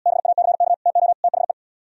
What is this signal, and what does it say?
Morse code, heard as a keyed tone, about 77 wpm, 700 Hz. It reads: BACK TO THE